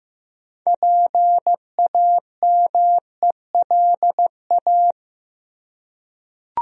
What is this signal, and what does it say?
Morse code, heard as a keyed tone, about 15 wpm, 700 Hz. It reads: PAMELA
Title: USA Common Names - 15wpm